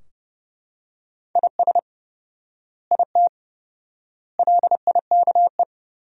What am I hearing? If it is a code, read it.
Morse code, heard as a keyed tone, about 32 wpm, 700 Hz. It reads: IS IT LIKE
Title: 4x Word-Spacing - Sentences from Top 100 Words - 30wpm